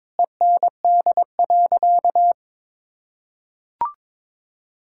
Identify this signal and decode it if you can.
Morse code, heard as a keyed tone, about 22 wpm, 700 Hz. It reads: END.
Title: Sentences from Top 1000 Words - Review Part 6 - 22wpm